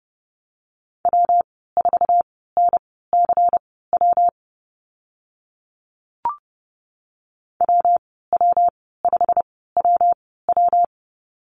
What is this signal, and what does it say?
Morse code, heard as a keyed tone, about 30 wpm, spaced as for 17 wpm, 700 Hz. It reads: W4NCW WW5WW
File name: Lesson 19g - Letter w - Call Signs 30wpm